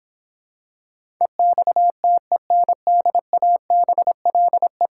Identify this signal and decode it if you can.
Morse code, heard as a keyed tone, about 25 wpm, 700 Hz. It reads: EXTENDABLE
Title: Difficult Words - Speed-Racing with Hints - 20wpm